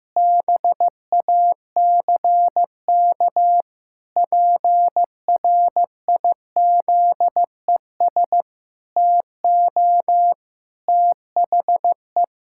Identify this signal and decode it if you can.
Morse code, heard as a keyed tone, about 14 wpm, 700 Hz. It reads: BACK PRIZES TO THE
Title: Pangram Challenge v2 - 15wpm